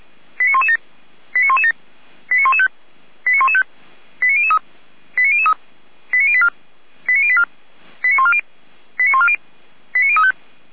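Five-tone selective call (ZVEI), heard as quick runs of five tones, tones of about 70 ms. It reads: 89117 89117 89115 89115 89003 89003 89084 89084 89140 89140 89035
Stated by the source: ZVEI five-tone selective call received on an SDR.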